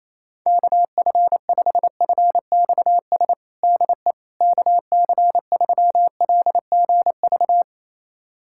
Morse code, heard as a keyed tone, about 28 wpm, 700 Hz. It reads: KF5FXS DE KC3LGV
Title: USA Callsign DE Callsign - 6 Characters - 28wpm